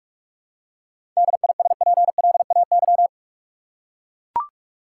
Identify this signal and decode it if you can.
Morse code, heard as a keyed tone, about 44 wpm, 700 Hz. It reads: DISPLAY
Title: Seven Letter Words - 45wpm